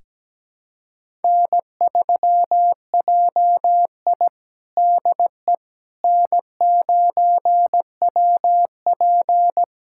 Morse code, heard as a keyed tone, about 17 wpm, 700 Hz. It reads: N3JI DE N9WP
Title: USA Callsign DE Callsign - 4 Characters - 17wpm